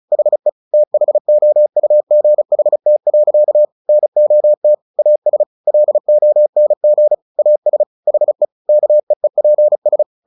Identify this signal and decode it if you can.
Morse code, heard as a keyed tone, about 34 wpm, 600 Hz. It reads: HE THOUGHT. NOT AS LONG AS HE KEEPS
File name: the old man and the sea / the old man and the sea 1 35wpm